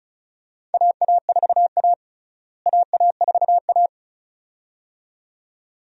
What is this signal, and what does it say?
Morse code, heard as a keyed tone, about 34 wpm, 700 Hz. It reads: AA4A AA4A
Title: USA Call Signs - 4 Characters - Repeated 2x - 35wpm